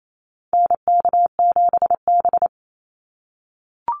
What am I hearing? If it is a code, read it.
Morse code, heard as a keyed tone, about 28 wpm, 700 Hz. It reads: NK7B
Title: Call Signs - 28wpm